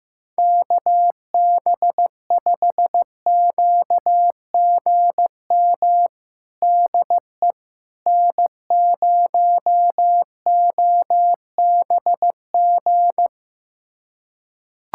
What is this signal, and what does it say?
Morse code, heard as a keyed tone, about 15 wpm, 700 Hz. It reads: KB5QGM DE N0OBG
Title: USA Callsign DE Callsign 15wpm